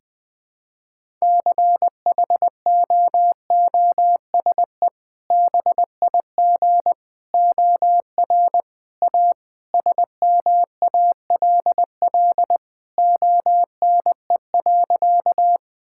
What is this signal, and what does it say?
Morse code, heard as a keyed tone, about 20 wpm, 700 Hz. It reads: CHOOSE BIG OR A SMALL ONE.
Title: Sentences with 2-Word Phrases - Spoken Beginning - 20wpm